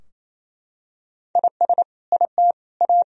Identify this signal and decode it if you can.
Morse code, heard as a keyed tone, about 27 wpm, 700 Hz. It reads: IS IT A
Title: Sentences from Top 1000 Words - Review Part 2 - 28wpm